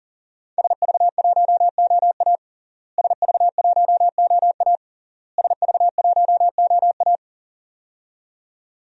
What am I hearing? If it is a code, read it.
Morse code, heard as a keyed tone, about 40 wpm, 700 Hz. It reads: SV1OA SV1OA SV1OA
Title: DXCC-100 Call Signs - 5 Characters - Repeated 3x - 40wpm